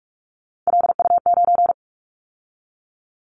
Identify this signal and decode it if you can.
Morse code, heard as a keyed tone, about 46 wpm, 690 Hz. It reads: LU9
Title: Callsign Prefixes for Top 50 DXCC - 45wpm